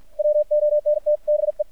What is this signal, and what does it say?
Morse code, heard as a keyed tone, about 50 wpm, 600 Hz. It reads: WONT BE
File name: archived / CRSnet